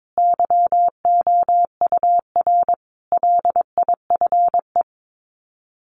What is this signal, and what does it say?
Morse code, heard as a keyed tone, about 22 wpm, 700 Hz. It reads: YOUR LIFE
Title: Sets of 5 Words 22wpm